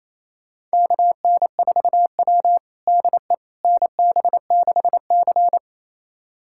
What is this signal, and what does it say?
Morse code, heard as a keyed tone, about 27 wpm, 690 Hz. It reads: KN4W DE NB6C
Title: USA Callsign DE Callsign - 4 Characters - 28wpm